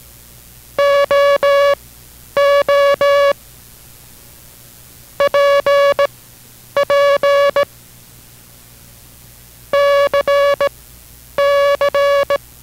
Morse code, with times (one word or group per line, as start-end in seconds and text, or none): 0.8-3.3 OO
5.2-7.6 PP
9.7-12.4 CC